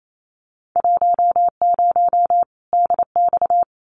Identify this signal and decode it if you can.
Morse code, heard as a keyed tone, about 28 wpm, 700 Hz. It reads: 10 DX
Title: Winter Field Day Exchanges 28wpm